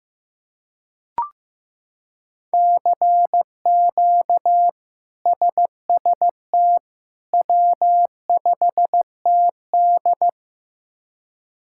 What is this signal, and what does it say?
Morse code, heard as a keyed tone, about 15 wpm, 700 Hz. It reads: CQ SST W5TD
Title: K1USN SST QSO Practice - 15wpm